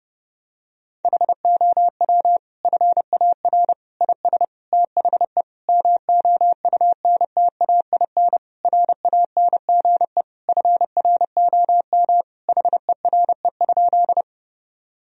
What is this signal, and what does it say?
Morse code, heard as a keyed tone, about 30 wpm, 700 Hz. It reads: HOW FAR IS THE MOUNTAIN RANGE FROM HERE?
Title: Sentences from Top 1000 Words - Review Part 4 - 30wpm